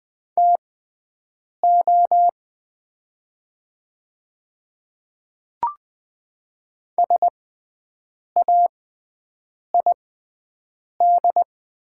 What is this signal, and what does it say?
Morse code, heard as a keyed tone, about 20 wpm, spaced as for 7 wpm, 700 Hz. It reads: TO SAID
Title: 6x Character Spacing - Top 100 Words - 20wpm